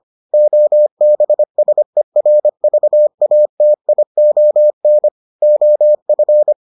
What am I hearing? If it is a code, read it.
Morse code, heard as a keyed tone, about 25 wpm, 600 Hz. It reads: OBSERVATION OF